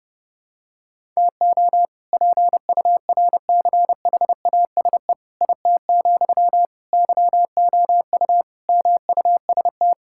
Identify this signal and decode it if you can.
Morse code, heard as a keyed tone, about 32 wpm, 700 Hz. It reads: TO PURCHASE IT, YOU MUST
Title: Sentences from Top 1000 Words - Review Part 5 - 30wpm